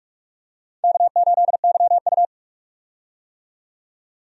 Morse code, heard as a keyed tone, about 42 wpm, 700 Hz. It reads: K8YU
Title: USA Call Signs - 4 Characters - 45wpm